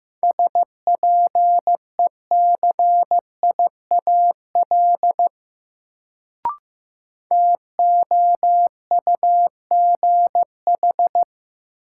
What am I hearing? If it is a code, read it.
Morse code, heard as a keyed tone, about 15 wpm, 700 Hz. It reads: SPECIAL TOUGH
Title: Sentences from Top 1200 Words - New Word Drill Alt - 15wpm